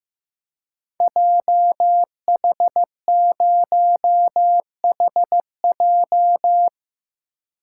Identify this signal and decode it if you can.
Morse code, heard as a keyed tone, about 15 wpm, 700 Hz. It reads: JH0HJ
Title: Code Groups - 5 Letter:Numbers - 15wpm